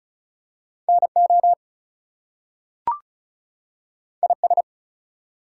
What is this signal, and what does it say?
Morse code, heard as a keyed tone, about 36 wpm, 700 Hz. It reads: NO IS
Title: Two Letter Words - 35wpm